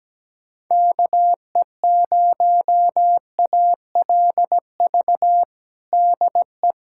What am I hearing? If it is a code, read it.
Morse code, heard as a keyed tone, about 17 wpm, 700 Hz. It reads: KE0ALV DE
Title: USA Callsign DE Callsign - 6 Characters - 17wpm